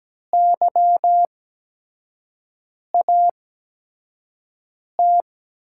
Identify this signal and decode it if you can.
Morse code, heard as a keyed tone, about 17 wpm, spaced as for 5 wpm, 700 Hz. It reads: YAT
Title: USA Call Signs - 5 Characters - 8x Character Spacing - 17wpm